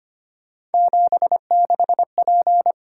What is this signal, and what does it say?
Morse code, heard as a keyed tone, about 25 wpm, 700 Hz. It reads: 76P
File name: Three Letter-Number Combinations - 25wpm